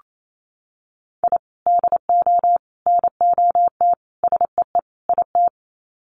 Morse code, heard as a keyed tone, about 28 wpm, 700 Hz. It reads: I DO NOT SEE IT
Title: Sets of 5 Words Encore2 28wpm